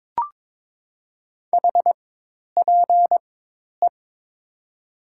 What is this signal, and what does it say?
Morse code, heard as a keyed tone, about 21 wpm, spaced as for 10 wpm, 700 Hz. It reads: HPE
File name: Top 90 QSO Elements - 4x Character Spacing - 22wpm